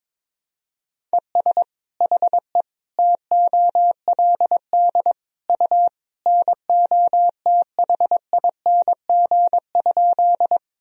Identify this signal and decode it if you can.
Morse code, heard as a keyed tone, about 22 wpm, 700 Hz. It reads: ES HE TOLD U NOTHING?
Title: Sentences from Top 500 Words and Abbreviations 22wpm